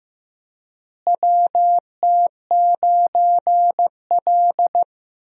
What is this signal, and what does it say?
Morse code, heard as a keyed tone, about 15 wpm, 700 Hz. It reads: WT9L
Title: USA Call Signs - 4 Characters - Repeated 3x - 15wpm